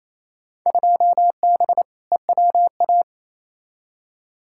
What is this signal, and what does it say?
Morse code, heard as a keyed tone, about 28 wpm, 700 Hz. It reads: 2B EWA
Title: Field Day Exchanges Encore - 28wpm